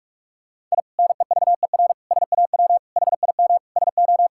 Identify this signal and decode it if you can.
Morse code, heard as a keyed tone, about 46 wpm, 700 Hz. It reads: I NEVER SAW HIM SO
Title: Sentences from Top 1000 Words - Review Part 6 - 45wpm